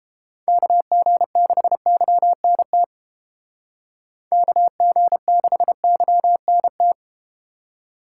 Morse code, heard as a keyed tone, about 32 wpm, 700 Hz. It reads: KG6YNT KG6YNT